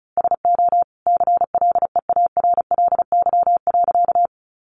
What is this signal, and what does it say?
Morse code, heard as a keyed tone, about 34 wpm, 700 Hz. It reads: SO CLEARLY.
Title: Sentences from Top 1300 Words 35wpm